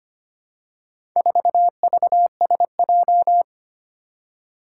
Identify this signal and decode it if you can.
Morse code, heard as a keyed tone, about 25 wpm, 700 Hz. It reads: 4VSJ
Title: Four Challenging Letter-Number Combinations 25wpm